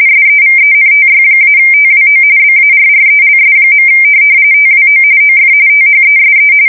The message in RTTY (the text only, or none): SEVENTY-TWO BARRELS. TWO OF THESE